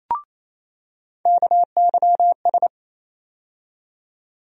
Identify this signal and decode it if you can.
Morse code, heard as a keyed tone, about 28 wpm, 700 Hz. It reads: KYS
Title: USA Call Sign Postfixes - 3 Letters 28wpm